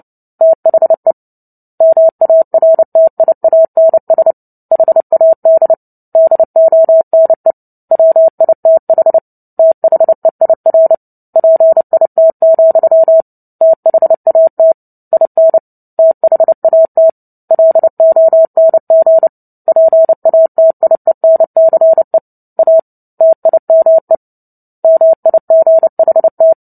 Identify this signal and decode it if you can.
Morse code, heard as a keyed tone, about 30 wpm, 650 Hz. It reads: THE MARTIANS HAD DONE WITH THEIR PIT, THAT IN THAT LONG PATIENCE A TIME MIGHT
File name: Book Two / 03-Chapter 3 - 29 WPM